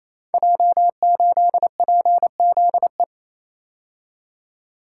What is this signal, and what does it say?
Morse code, heard as a keyed tone, about 29 wpm, 700 Hz. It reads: J8PZE
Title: Code Groups - 5 Letter:Numbers - 28wpm